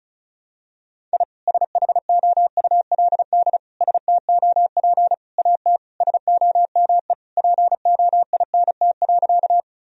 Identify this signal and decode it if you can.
Morse code, heard as a keyed tone, about 36 wpm, 700 Hz. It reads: I SHOULD STOP AT SOME POINT.
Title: Sentences from Top 1000 Words - Review Part 1 - 35wpm